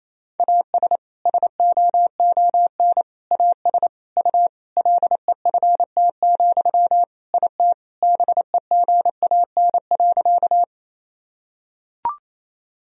Morse code, heard as a keyed tone, about 28 wpm, 700 Hz. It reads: AS SOON AS U LEFT, IT BEGAN.
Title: Sentences from Top 400 Words and Abbreviations 28wpm